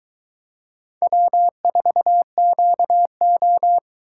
Morse code, heard as a keyed tone, about 21 wpm, 700 Hz. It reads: W4QO